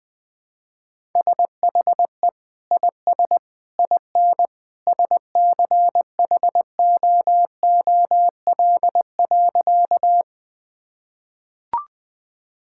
Morse code, heard as a keyed tone, about 19 wpm, 700 Hz. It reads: SHE IS IN SCHOOL.
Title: Sentences with 2-Word Phrases - Spoken Beginning - 20wpm